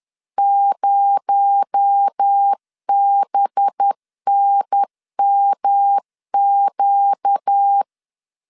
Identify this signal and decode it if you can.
Morse code, heard as a keyed tone, about 10 wpm, 800 Hz. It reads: 0BNMQ
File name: LEZIONI / 6 PKX0 / 6 PKX0 10wpm 2707